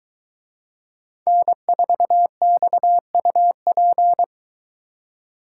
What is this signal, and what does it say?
Morse code, heard as a keyed tone, about 22 wpm, 700 Hz. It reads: N4XUP